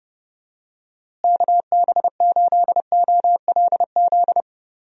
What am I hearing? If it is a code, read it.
Morse code, heard as a keyed tone, about 30 wpm, 700 Hz. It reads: KB8OLZ
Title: USA Call Signs - 6 Characters - Repeated 3x - 30wpm